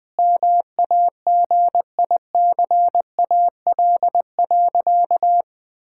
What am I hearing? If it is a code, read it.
Morse code, heard as a keyed tone, about 20 wpm, 700 Hz. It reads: MAGICAL.
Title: Sentences with 5-Word Phrases - Spoken Beginning - 20wpm